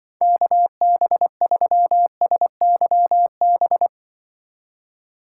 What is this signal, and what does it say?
Morse code, heard as a keyed tone, about 24 wpm, 700 Hz. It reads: KB3SYB
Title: Speed Racing - USA Call Signs 20wpm